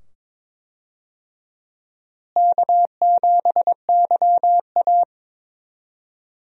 Morse code, heard as a keyed tone, about 22 wpm, 700 Hz. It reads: K7YA
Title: USA Call Signs - 4 Characters - Repeated 2x - 22wpm